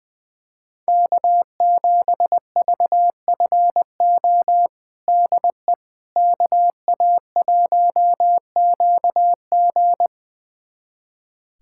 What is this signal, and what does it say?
Morse code, heard as a keyed tone, about 21 wpm, 700 Hz. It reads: K7VFO DE KA1QG
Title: USA Callsign DE Callsign - 5 Characters - 20wpm